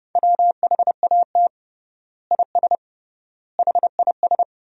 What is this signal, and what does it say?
Morse code, heard as a keyed tone, about 31 wpm, 700 Hz. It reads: WHAT IS HIS
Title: 3x Word-Spacing - Sentences from Top 200 Words - 30wpm